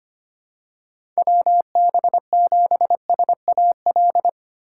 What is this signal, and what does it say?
Morse code, heard as a keyed tone, about 25 wpm, 700 Hz. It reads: WB7SAL
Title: USA Call Signs - 6 Characters - 2x Character Spacing - 25wpm